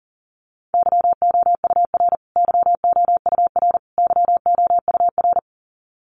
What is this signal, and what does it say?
Morse code, heard as a keyed tone, about 40 wpm, 700 Hz. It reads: YOUR YOUR YOUR
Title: Top 500 Words Repeated 40wpm